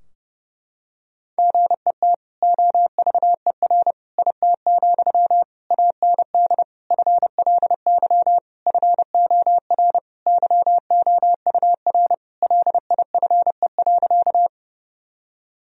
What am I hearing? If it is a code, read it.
Morse code, heard as a keyed tone, about 32 wpm, 700 Hz. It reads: GET OVER IT, AND FLY FOR YOUR LIFE.
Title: Sentences from Top 1000 Words - Review Part 2 - 30wpm